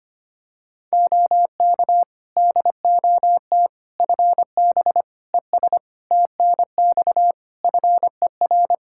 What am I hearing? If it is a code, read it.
Morse code, heard as a keyed tone, about 25 wpm, 700 Hz. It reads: OK DOT FB ES TNX FER
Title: Practice QSO - Exchange 2 - Info - 25wpm